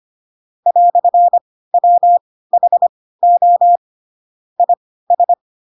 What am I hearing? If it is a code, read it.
Morse code, heard as a keyed tone, about 25 wpm, spaced as for 16 wpm, 700 Hz. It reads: "WHO IS
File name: Pinocchio / Pinocchio ch4 10 25 ik1hge